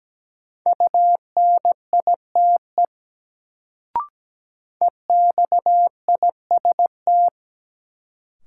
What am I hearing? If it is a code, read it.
Morse code, heard as a keyed tone, about 17 wpm, 700 Hz. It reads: UNITE EXIST
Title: Sentences from Top 1000 Words - Review Part 5 - New Word Drill - 17wpm